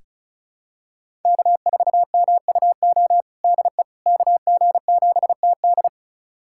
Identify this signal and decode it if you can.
Morse code, heard as a keyed tone, about 36 wpm, 700 Hz. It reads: K4MUO DE KG7TD
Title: USA Callsign DE Callsign - 5 Characters - 35wpm